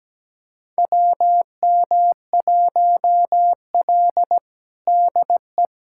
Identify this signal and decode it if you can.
Morse code, heard as a keyed tone, about 17 wpm, 700 Hz. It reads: WM1L DE
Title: USA Callsign DE Callsign - 4 Characters - 17wpm